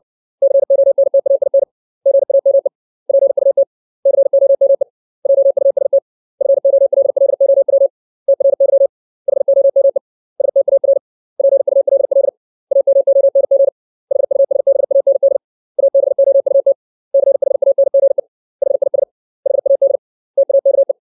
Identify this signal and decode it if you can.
Morse code, heard as a keyed tone, about 60 wpm, 550 Hz. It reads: CONTAIN GAME PUT COME JUST FOLLOW TRY HOME STAR PULL AMONG HUSBAND ABOUT CHANGE HIS HAD TAKE
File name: 632 words / 632 Words 60WPM